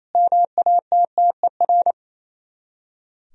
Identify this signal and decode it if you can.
Morse code, heard as a keyed tone, about 35 wpm, 700 Hz. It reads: MATTER